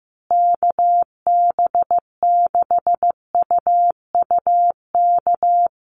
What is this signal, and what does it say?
Morse code, heard as a keyed tone, about 15 wpm, 700 Hz. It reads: KB6UUK
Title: USA Call Signs Repeated 3x 15wpm